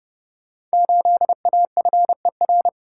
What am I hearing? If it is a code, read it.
Morse code, heard as a keyed tone, about 30 wpm, 700 Hz. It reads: 8AFER
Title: Code Groups - 5 Letter:Numbers - 30wpm